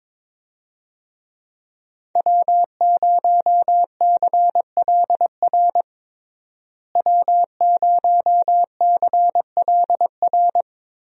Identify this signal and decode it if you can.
Morse code, heard as a keyed tone, about 21 wpm, 700 Hz. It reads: W0CLR W0CLR
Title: USA Call Signs - 5 Characters - Repeated 3x - 22wpm